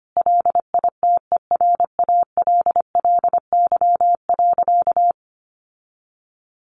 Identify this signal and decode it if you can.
Morse code, heard as a keyed tone, about 25 wpm, 700 Hz. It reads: LITERALLY.
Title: Sentences from Top 1400 Words 25wpm